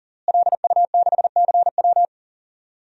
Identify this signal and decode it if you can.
Morse code, heard as a keyed tone, about 40 wpm, 700 Hz. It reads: LU6CW
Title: DXCC-100 Call Signs - 5 Characters - Repeated 2x - 40wpm